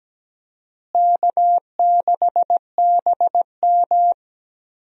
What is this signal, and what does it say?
Morse code, heard as a keyed tone, about 17 wpm, 700 Hz. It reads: K6BM
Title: USA Call Signs - 4 Characters - Repeated 2x - 17wpm